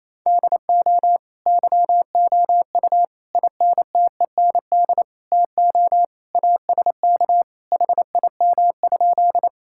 Morse code, heard as a keyed tone, about 28 wpm, 700 Hz. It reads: DO YOU INTEND TO ASK HIM?
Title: Sentences from Top 1600 Words 28wpm